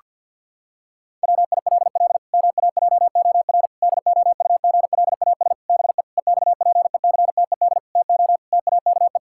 Morse code, heard as a keyed tone, about 50 wpm, 700 Hz. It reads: WILL MAJOR DOUGLAS BE EXPECTED TO TAKE
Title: Pangram Challenge v2 - 50wpm